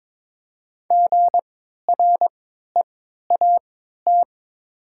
Morse code, heard as a keyed tone, about 22 wpm, spaced as for 13 wpm, 700 Hz. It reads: GREAT
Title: 3x Character Spacing - Top 300 Words - 22wpm